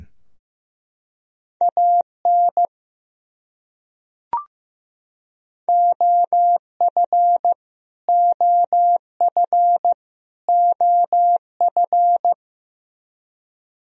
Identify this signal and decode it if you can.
Morse code, heard as a keyed tone, about 15 wpm, 700 Hz. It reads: AN OF OF OF
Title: Top 100 Words Repeated 15wpm